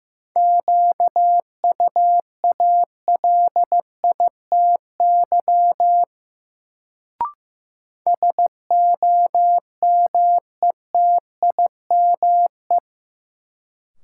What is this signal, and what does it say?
Morse code, heard as a keyed tone, about 15 wpm, 700 Hz. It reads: QUALITY SOMETIME
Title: Sentences from Top 1500 Words - New Word Drill Alt - 15wpm